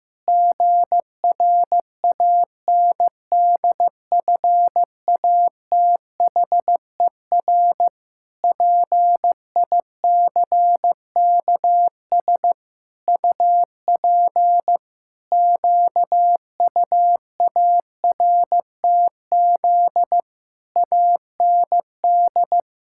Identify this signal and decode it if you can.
Morse code, heard as a keyed tone, about 15 wpm, 700 Hz. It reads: GRANDFATHER PICKS UP QUARTZ AND
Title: Pangram Challenge v2 - 15wpm